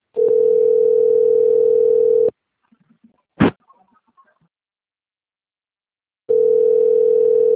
Ringback tone (North America), 440 + 480 Hz.
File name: rg-1600-unknown- - - .2627